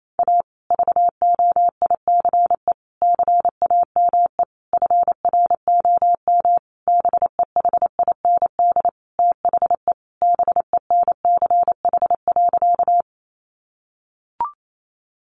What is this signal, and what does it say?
Morse code, heard as a keyed tone, about 28 wpm, 700 Hz. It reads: A VOICE CAME FROM BEHIND THE BENCH.
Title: Sentences from Top 1400 Words 28wpm